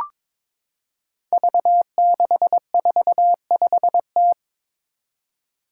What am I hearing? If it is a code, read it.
Morse code, heard as a keyed tone, about 22 wpm, 700 Hz. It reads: V645T
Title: Five Challenging Letter-Number Combinations 22wpm